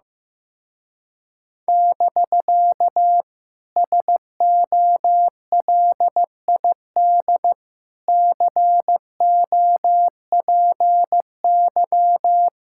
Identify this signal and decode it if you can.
Morse code, heard as a keyed tone, about 15 wpm, 700 Hz. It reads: <BK> SOLID COPY
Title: Speed-Racing Common QSO Phrases Encore - 15wpm